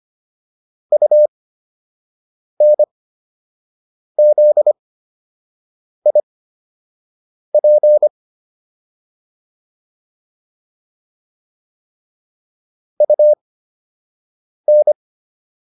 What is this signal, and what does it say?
Morse code, heard as a keyed tone, about 25 wpm, spaced as for 6 wpm, 600 Hz. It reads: UNZIP UN
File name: UN5 06